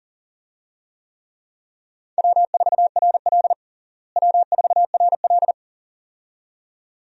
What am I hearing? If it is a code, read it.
Morse code, heard as a keyed tone, about 40 wpm, 700 Hz. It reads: W4RL W4RL
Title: USA Call Signs - 4 Characters - Repeated 2x - 40wpm